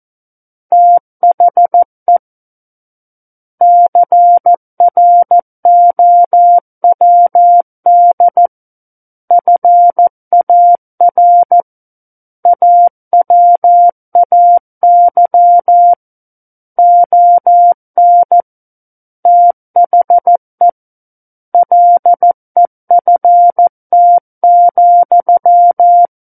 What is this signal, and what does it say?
Morse code, heard as a keyed tone, about 14 wpm, 700 Hz. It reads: THE CROWD FAR AWAY ON THE LEFT,
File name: Book One / 05-Chapter 5 - 14 WPM